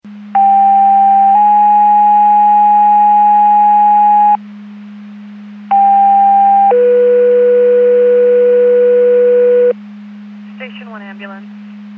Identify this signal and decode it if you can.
Two-tone page, heard as a one-second tone, then a three-second tone, about 790 then 830 Hz, about 800 then 490 Hz. It is heard from 0.4 to 9.7 s.
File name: Portage 154.385 Paging